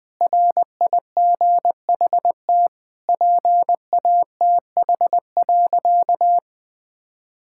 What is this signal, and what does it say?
Morse code, heard as a keyed tone, about 21 wpm, 700 Hz. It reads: RIGHT PATH.
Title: Sentences with 2-Word Phrases - 20wpm